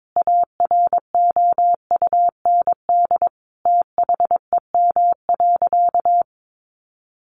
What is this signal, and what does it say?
Morse code, heard as a keyed tone, about 22 wpm, 700 Hz. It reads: AROUND THEM.
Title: Sentences from Top 600 Words 22wpm